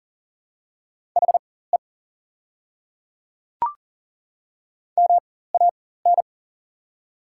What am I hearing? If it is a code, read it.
Morse code, heard as a keyed tone, about 40 wpm, spaced as for 19 wpm, 700 Hz. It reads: HE MAN
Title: 4x Character Spacing - Top 200 Words - 40wpm